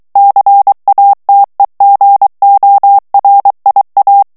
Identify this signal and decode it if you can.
Morse code, heard as a keyed tone, about 23 wpm, 800 Hz. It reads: CATEGORIA
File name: ITALIEN texte clair 2 vitesse 22 mots minute